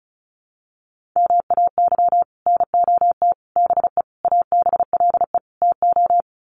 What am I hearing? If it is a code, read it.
Morse code, heard as a keyed tone, about 34 wpm, 700 Hz. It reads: MAY NOT BE ABLE TO
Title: Sets of 5 Words 35wpm